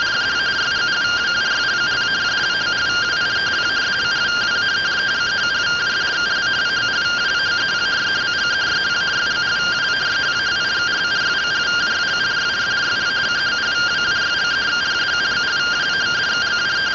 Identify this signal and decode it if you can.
Two-tone teleprinter signal, about 170 Hz shift.
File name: findings / CT 1 Scout 0001.gen-02 rtty rx